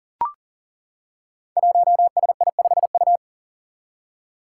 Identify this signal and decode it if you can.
Morse code, heard as a keyed tone, about 40 wpm, 700 Hz. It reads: 1SI5U
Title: Five Challenging Letter-Number Combinations 40wpm